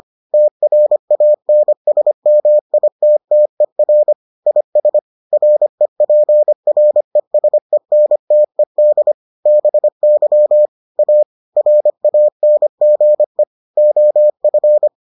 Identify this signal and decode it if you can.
Morse code, heard as a keyed tone, about 25 wpm, 600 Hz. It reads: TRANSMITTER IS REPRESENTED BY A RANGE OF